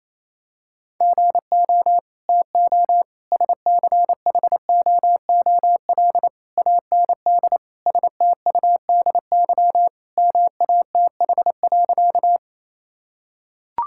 Morse code, heard as a keyed tone, about 28 wpm, 700 Hz. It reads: GO TO SCHOOL AND STUDY MATH.
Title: Sentences from Top 1000 Words - Review Part 6 - 28wpm